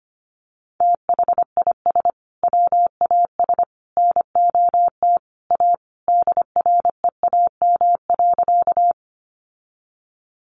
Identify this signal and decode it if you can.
Morse code, heard as a keyed tone, about 25 wpm, 700 Hz. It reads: THIS WAS NOT A DREAM.
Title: Sentences from Top 1500 Words 25wpm